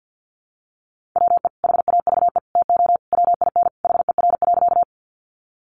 Morse code, heard as a keyed tone, about 50 wpm, 700 Hz. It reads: WE HAVE TO WIN HER.
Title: Sentences from Top 1500 Words 50wpm